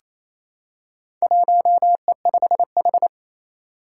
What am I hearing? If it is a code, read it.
Morse code, heard as a keyed tone, about 27 wpm, 700 Hz. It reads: 1E5H
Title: Four Challenging Letter-Number Combinations 28wpm